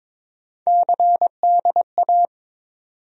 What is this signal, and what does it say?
Morse code, heard as a keyed tone, about 22 wpm, 700 Hz. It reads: CDA
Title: USA Call Sign Postfixes - 3 Letters 22wpm